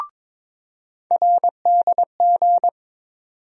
Morse code, heard as a keyed tone, about 22 wpm, 700 Hz. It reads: RDG
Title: USA Call Sign Postfixes - 3 Letters 22wpm